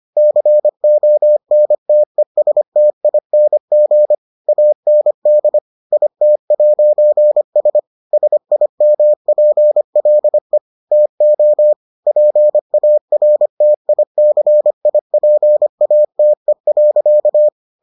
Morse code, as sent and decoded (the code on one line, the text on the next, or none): -.-. --- -. - . ... - .. -. --. / .- -. -.. / .. - .----. ... / ... .. -- .--. .-.. / - --- / .--. .- .-. - .. -.-. .. .--. .- - . .-.-.-
CONTESTING AND IT'S SIMPLE TO PARTICIPATE.